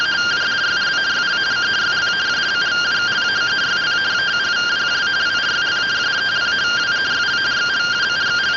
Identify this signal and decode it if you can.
Two-tone teleprinter signal, about 170 Hz shift.